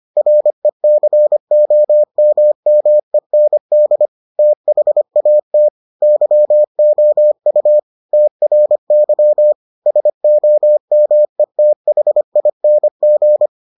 Morse code, heard as a keyed tone, about 25 wpm, 600 Hz. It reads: RECOMMEND THAT YOU TRY SOMETHING